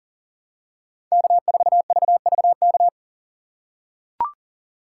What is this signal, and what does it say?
Morse code, heard as a keyed tone, about 37 wpm, 700 Hz. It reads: K4VVK
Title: USA Call Signs - 5 Characters - 2x Character Spacing - 40wpm